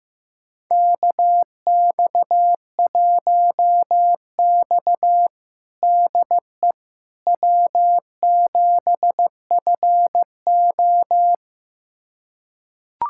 Morse code, heard as a keyed tone, about 15 wpm, 700 Hz. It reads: KX1X DE W7FO
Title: USA Callsign DE Callsign - 4 Characters - 15wpm